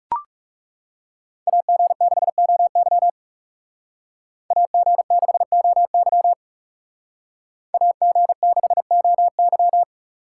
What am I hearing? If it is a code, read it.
Morse code, heard as a keyed tone, about 38 wpm, 700 Hz. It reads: AG6OY AG6OY AG6OY